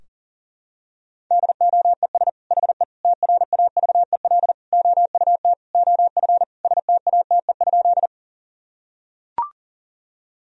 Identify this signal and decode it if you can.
Morse code, heard as a keyed tone, about 40 wpm, 700 Hz. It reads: DOES HE TRAVEL OUT OF STATE?
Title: Sentences from Top 1000 Words - Review Part 2 - 40wpm